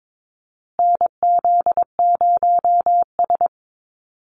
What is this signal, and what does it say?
Morse code, heard as a keyed tone, about 22 wpm, 700 Hz. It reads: NZ0S
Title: USA Call Signs Repeated 3x 22wpm